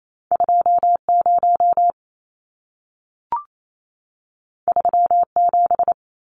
Morse code, heard as a keyed tone, about 28 wpm, 700 Hz. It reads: 20 37
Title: Numbers - 2-Digit 28wpm